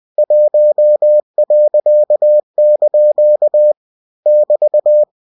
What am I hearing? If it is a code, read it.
Morse code, heard as a keyed tone, about 20 wpm, 600 Hz. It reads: 1.) <BT>
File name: cw-books / winnie-the-pooh / winnie-20wpm-page-0063